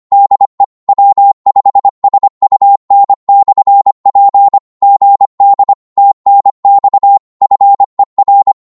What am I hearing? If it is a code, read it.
Morse code, heard as a keyed tone, about 25 wpm, 800 Hz. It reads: DE W5SUN/P GD TNX FER
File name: qso-25wpm